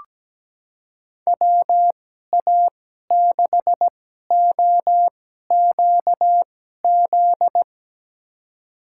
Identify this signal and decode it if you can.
Morse code, heard as a keyed tone, about 17 wpm, spaced as for 12 wpm, 700 Hz. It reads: WA6OQZ
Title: USA Call Signs - 2x Character Spacing - 17wpm